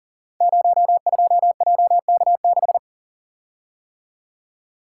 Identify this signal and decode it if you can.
Morse code, heard as a keyed tone, about 39 wpm, 700 Hz. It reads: 02JK6
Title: Code Groups - 5 Letter:Numbers - 40wpm